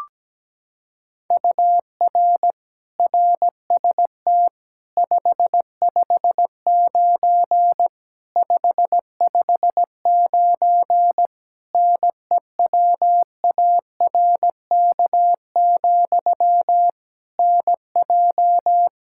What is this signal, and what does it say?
Morse code, heard as a keyed tone, about 17 wpm, 700 Hz. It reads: UR RST 559 559 NEWARK, NJ
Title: Practice QSO - Exchange 1 - Beginning - 17wpm